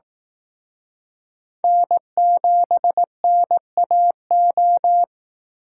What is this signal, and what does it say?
Morse code, heard as a keyed tone, about 18 wpm, 700 Hz. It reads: N7NAO
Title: Speed Racing - USA 1x3 Call Signs 15wpm